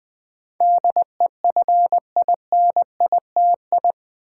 Morse code, heard as a keyed tone, about 20 wpm, 700 Hz. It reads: DEFINITI
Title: Build A Word - 4 Letters At A Time - 10-Letter Words - 20wpm